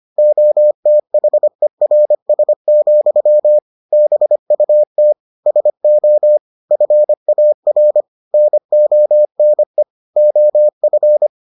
Morse code, heard as a keyed tone, about 25 wpm, 600 Hz. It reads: OTHERS, BUT SO FAR NONE OF